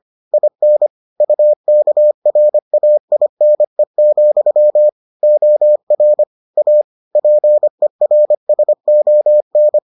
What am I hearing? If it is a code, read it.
Morse code, heard as a keyed tone, about 25 wpm, 600 Hz. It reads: IN UKRAINE, OR A PERSON